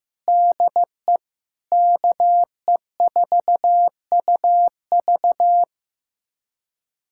Morse code, heard as a keyed tone, about 15 wpm, 700 Hz. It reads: DE KE4UV
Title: USA Callsign DE Callsign - 5 Characters - 15wpm